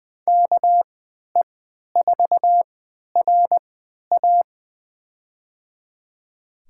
Morse code, heard as a keyed tone, about 20 wpm, spaced as for 11 wpm, 700 Hz. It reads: KE4RA
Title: USA Call Signs - 5 Characters - 3x Character Spacing - 20wpm